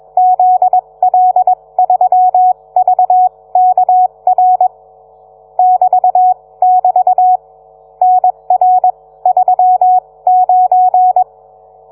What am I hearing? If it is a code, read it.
Morse code, heard as a keyed tone, about 21 wpm, spaced as for 18 wpm, 740 Hz. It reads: ZL3VKR <BT><BT> NR39